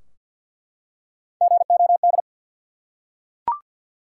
Morse code, heard as a keyed tone, about 48 wpm, 700 Hz. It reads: GOD
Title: Three Letter Words - 50wpm